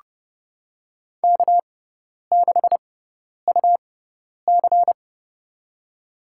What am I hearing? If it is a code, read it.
Morse code, heard as a keyed tone, about 30 wpm, spaced as for 10 wpm, 700 Hz. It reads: K6UC
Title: USA Call Signs - 4 Characters - 6x Character Spacing - 30wpm